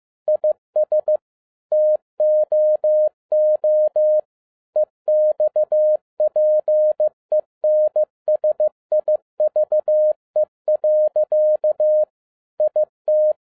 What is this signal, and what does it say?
Morse code, heard as a keyed tone, about 15 wpm, 610 Hz. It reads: IS TOO EXPENSIVE. IT
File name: 2011-07-mp3 / n12 T8XO 15wpm